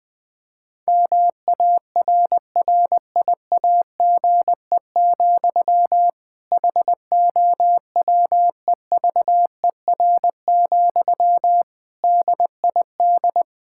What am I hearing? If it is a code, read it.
Morse code, heard as a keyed tone, about 20 wpm, 700 Hz. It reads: MARRIAGE, HOWEVER, DID
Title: Sentences from Top 1000 Words - Review Part 5 - 20wpm